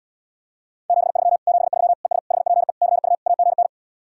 Morse code, heard as a keyed tone, about 77 wpm, 700 Hz. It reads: 73 73 ES HPE CU AGN